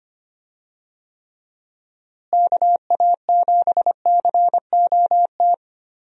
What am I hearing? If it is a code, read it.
Morse code, heard as a keyed tone, about 25 wpm, 700 Hz. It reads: KA7COT
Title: USA Call Signs - 6 Characters - Repeated 2x - 25wpm